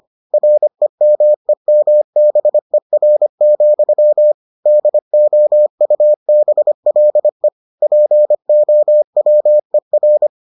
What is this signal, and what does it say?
Morse code, heard as a keyed tone, about 25 wpm, 600 Hz. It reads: REMEMBER, DOUBLE POWER